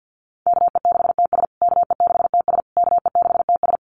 Morse code, heard as a keyed tone, about 50 wpm, 700 Hz. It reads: KE6TS KE6TS KE6TS
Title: USA Call Signs Repeated 3x 50wpm